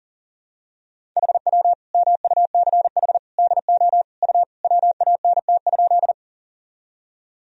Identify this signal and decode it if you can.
Morse code, heard as a keyed tone, about 40 wpm, 700 Hz. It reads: HW MUCH DO U WANT?
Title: Sentences from Top 200 Words and Abbreviations 40wpm